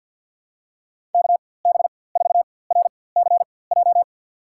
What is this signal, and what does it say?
Morse code, heard as a keyed tone, about 46 wpm, spaced as for 23 wpm, 700 Hz. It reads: KB4RCJ
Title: USA Call Signs - 6 Characters - 4x Character Spacing - 50wpm